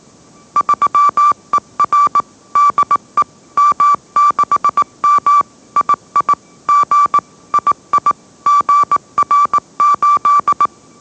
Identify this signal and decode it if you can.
Morse code, heard as a keyed tone, about 19 wpm, 1160 Hz. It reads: 3ER DE M6M II G II GR8